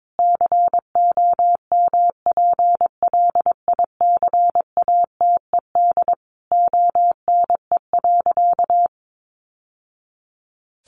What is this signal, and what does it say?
Morse code, heard as a keyed tone, about 22 wpm, 700 Hz. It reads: COMPLICATED ONE.
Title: Sentences from Top 1200 Words 22wpm